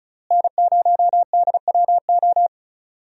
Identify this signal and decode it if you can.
Morse code, heard as a keyed tone, about 36 wpm, 700 Hz. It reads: N0DWO
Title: USA Call Signs - 5 Characters - Repeated 2x - 35wpm